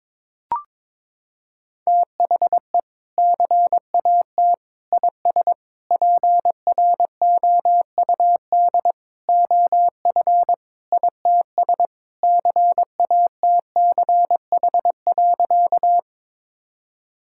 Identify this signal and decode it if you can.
Morse code, heard as a keyed tone, about 22 wpm, 700 Hz. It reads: THE CAT IS PROUD OF ITS CATCH.
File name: practice / 22wpm / Sentences with 2-Word Phrases - 22wpm